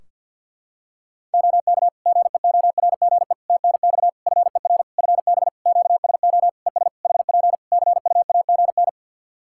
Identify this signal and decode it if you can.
Morse code, heard as a keyed tone, about 50 wpm, 700 Hz. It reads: OK GEORGE TNX FER FB QSO ES HP CUAGN
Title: K1USN SST QSO Practice 50wpm